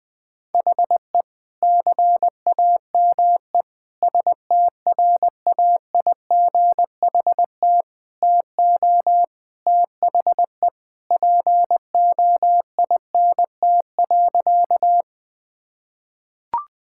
Morse code, heard as a keyed tone, about 20 wpm, 700 Hz. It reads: HE CAME STRAIGHT TO THE POINT.
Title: Sentences from Top 1000 Words - Review Part 6 - 20wpm